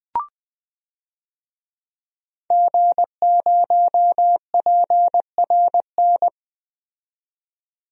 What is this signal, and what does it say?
Morse code, heard as a keyed tone, about 20 wpm, 700 Hz. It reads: G0PRN
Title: DXCC-100 Call Signs - 5 Characters - 20wpm